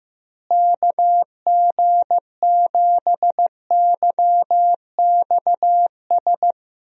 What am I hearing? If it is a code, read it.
Morse code, heard as a keyed tone, about 15 wpm, 700 Hz. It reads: KG7YXS
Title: USA Call Signs - 6 Characters - Repeated 3x - 15wpm